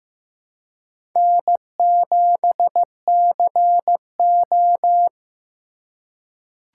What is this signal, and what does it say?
Morse code, heard as a keyed tone, about 15 wpm, 690 Hz. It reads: N7CO
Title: USA Call Signs - 4 Characters - 2x Character Spacing - 15wpm